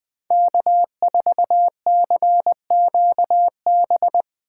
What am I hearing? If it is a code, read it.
Morse code, heard as a keyed tone, about 20 wpm, 700 Hz. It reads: K4CQB
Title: Speed Racing - USA 1x3 Call Signs 17wpm